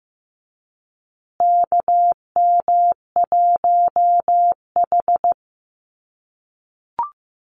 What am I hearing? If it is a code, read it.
Morse code, heard as a keyed tone, about 15 wpm, 700 Hz. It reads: KM1H
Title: USA Call Signs Repeated 3x 15wpm